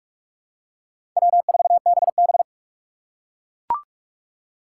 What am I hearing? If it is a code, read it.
Morse code, heard as a keyed tone, about 42 wpm, 700 Hz. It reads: W4BB
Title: USA Call Signs - 6x Character Spacing - 45wpm